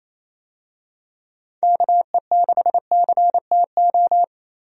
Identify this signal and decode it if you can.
Morse code, heard as a keyed tone, about 28 wpm, 700 Hz. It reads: KE6CTO
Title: USA Call Signs - 6 Characters - Repeated 2x - 28wpm